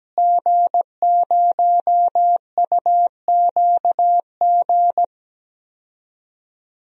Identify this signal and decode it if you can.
Morse code, heard as a keyed tone, about 17 wpm, 700 Hz. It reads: G0UQG
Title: DXCC-100 Call Signs - 5 Characters - Repeated 2x - 17wpm